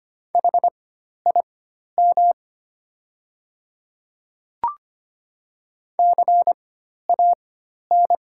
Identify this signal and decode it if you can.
Morse code, heard as a keyed tone, about 25 wpm, spaced as for 12 wpm, 700 Hz. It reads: HIM CAN
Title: 4x Character Spacing - Top 100 Words - 25wpm